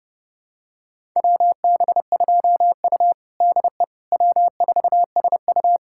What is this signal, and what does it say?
Morse code, heard as a keyed tone, about 30 wpm, 700 Hz. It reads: WB2U DE W4SU
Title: USA Callsign DE Callsign - 4 Characters - 30wpm